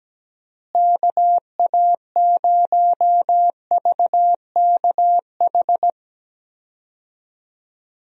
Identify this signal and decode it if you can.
Morse code, heard as a keyed tone, about 17 wpm, 700 Hz. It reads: KA0VKH
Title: USA Call Signs - 6 Characters - Repeated 3x - 17wpm